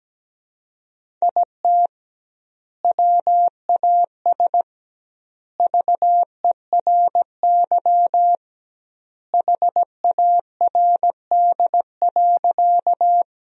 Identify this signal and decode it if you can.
Morse code, heard as a keyed tone, about 17 wpm, 700 Hz. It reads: IT WAS VERY HARD.
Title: 2x Word-Spacing - Sentences from Top 300 Words - 17wpm